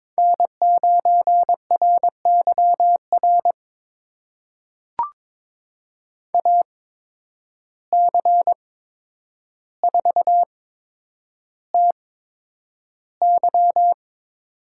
Morse code, heard as a keyed tone, about 22 wpm, spaced as for 9 wpm, 700 Hz. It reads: N9RYR AC4TY
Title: USA Call Signs - 5 Characters - 8x Character Spacing - 22wpm